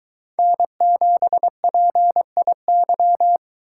Morse code, heard as a keyed tone, about 23 wpm, 700 Hz. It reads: N7PIY